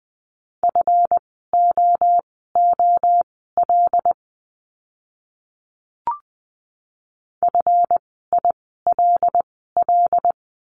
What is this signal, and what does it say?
Morse code, heard as a keyed tone, about 20 wpm, spaced as for 15 wpm, 700 Hz. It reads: FOOL FILL
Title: 139 Lesson 23f - Letter f - Words 20wpm